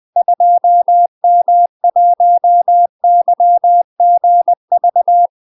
Morse code, heard as a keyed tone, about 20 wpm, 700 Hz. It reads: 2M1YGV
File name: UK MASTER QSOs TRAINING / 20WPM / UK MASTER QSOs 0010